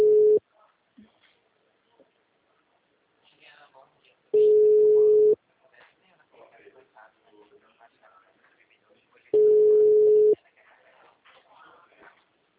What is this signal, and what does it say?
Ringback tone (Europe, 425 Hz), with speech.